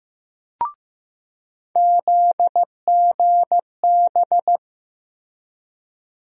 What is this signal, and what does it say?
Morse code, heard as a keyed tone, about 15 wpm, 700 Hz. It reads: ZGB
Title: USA Call Sign Postfixes - 3 Letters 15wpm